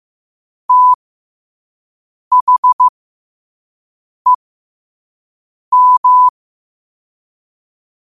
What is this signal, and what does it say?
Morse code, heard as a keyed tone, about 15 wpm, spaced as for 5 wpm, 1000 Hz. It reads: THEM
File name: qotdfiles / 5wpm